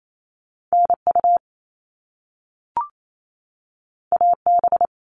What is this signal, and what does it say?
Morse code, heard as a keyed tone, about 28 wpm, 700 Hz. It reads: NU AB
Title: Top Two Letter Combinations in Words 28wpm